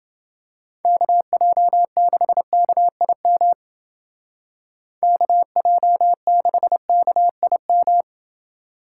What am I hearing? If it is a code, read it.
Morse code, heard as a keyed tone, about 28 wpm, 700 Hz. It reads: KJ6KIM KJ6KIM